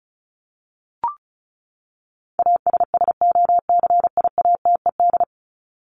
Morse code, heard as a keyed tone, about 34 wpm, 700 Hz. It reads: ASSOCIATED
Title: ED Suffix - Top 100 Words with Base Words 35wpm